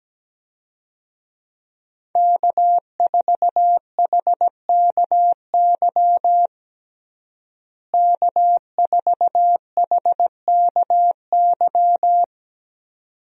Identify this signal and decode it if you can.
Morse code, heard as a keyed tone, about 17 wpm, 700 Hz. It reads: K4HKY K4HKY
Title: USA Call Signs - 5 Characters - Repeated 2x - 17wpm